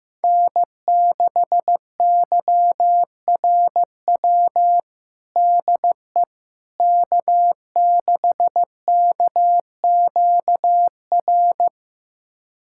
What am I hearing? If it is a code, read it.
Morse code, heard as a keyed tone, about 16 wpm, 700 Hz. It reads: N6YRW DE K6KQR
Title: USA Callsign DE Callsign - 5 Characters - 15wpm